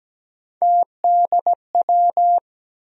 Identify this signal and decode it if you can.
Morse code, heard as a keyed tone, about 17 wpm, 700 Hz. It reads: TDW
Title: USA Call Sign Postfixes - 3 Letters 17wpm